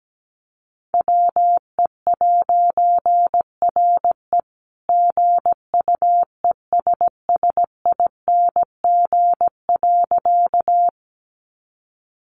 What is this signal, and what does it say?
Morse code, heard as a keyed tone, about 17 wpm, 700 Hz. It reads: WE'RE GUESSING.
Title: A Sound Of Thunder 17wpm